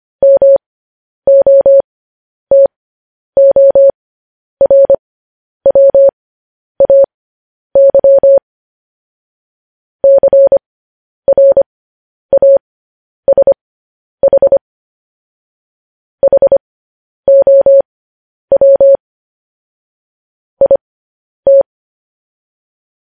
Morse code, heard as a keyed tone, about 25 wpm, spaced as for 10 wpm, 550 Hz. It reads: MOTORWAY CRASH HOW IT
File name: podcast / News headlines-1x-25x10